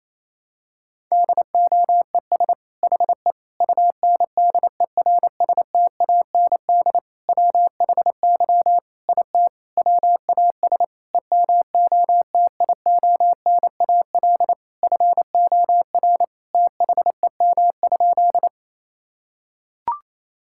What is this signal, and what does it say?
Morse code, heard as a keyed tone, about 27 wpm, 700 Hz. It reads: DOES HE UNDERSTAND WHY IT WAS EMOTIONAL FOR THEM?
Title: Sentences from Top 1000 Words - Review Part 5 - 28wpm